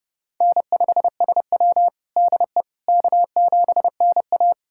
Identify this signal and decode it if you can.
Morse code, heard as a keyed tone, about 31 wpm, 700 Hz. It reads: N5SW DE K7NA
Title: USA Callsign DE Callsign - 4 Characters - 30wpm